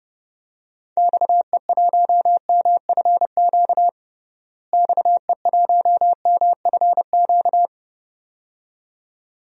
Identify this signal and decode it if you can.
Morse code, heard as a keyed tone, about 30 wpm, 700 Hz. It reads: XE1MFQ XE1MFQ
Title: DXCC-100 Call Signs - 6 Characters - Repeated 2x - 30wpm